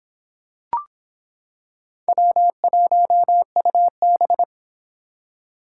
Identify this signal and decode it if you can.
Morse code, heard as a keyed tone, about 26 wpm, 700 Hz. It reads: W1UB